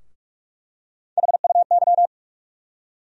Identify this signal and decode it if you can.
Morse code, heard as a keyed tone, about 46 wpm, 700 Hz. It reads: HUY
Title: USA Call Sign Postfixes - 3 Letters 45wpm